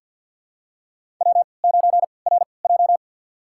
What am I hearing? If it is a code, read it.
Morse code, heard as a keyed tone, about 50 wpm, spaced as for 28 wpm, 700 Hz. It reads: W9RJ
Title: USA Call Signs - 4 Characters - 3x Character Spacing - 50wpm